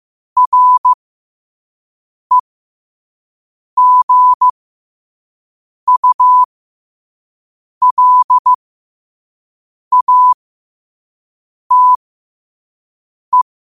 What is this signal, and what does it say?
Morse code, heard as a keyed tone, about 15 wpm, spaced as for 5 wpm, 1000 Hz. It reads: REGULATE
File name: qotdfiles / 5wpm